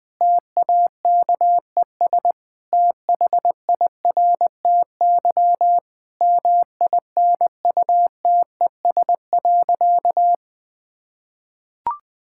Morse code, heard as a keyed tone, about 20 wpm, 700 Hz. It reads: TAKES THIRTY MINUTES.
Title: Difficult Sentences with Hints - 20wpm